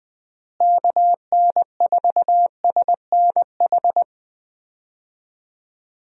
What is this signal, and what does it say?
Morse code, heard as a keyed tone, about 20 wpm, 700 Hz. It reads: KN4SNH
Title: USA Call Signs - 6 Characters - Repeated 2x - 20wpm